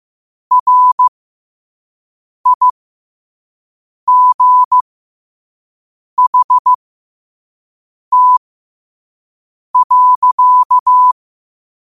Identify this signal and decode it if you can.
Morse code, heard as a keyed tone, about 15 wpm, spaced as for 5 wpm, 1000 Hz. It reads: RIGHT.
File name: qotdfiles / 5wpm